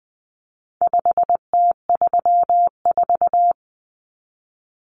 Morse code, heard as a keyed tone, about 20 wpm, 700 Hz. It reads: 5T34
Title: Common Cut Numbers - 4-Digit 20wpm